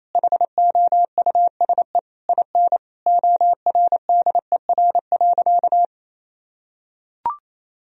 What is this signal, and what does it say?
Morse code, heard as a keyed tone, about 28 wpm, 700 Hz. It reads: HOUSE IN ORDER.
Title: Sentences from Top 1000 Words - Review Part 2 - 28wpm